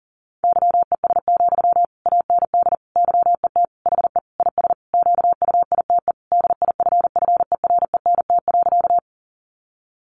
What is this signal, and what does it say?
Morse code, heard as a keyed tone, about 40 wpm, 700 Hz. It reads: YES, AND YET HE IS QUITE DIFFERENT.
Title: Sentences from Top 700 Words 40wpm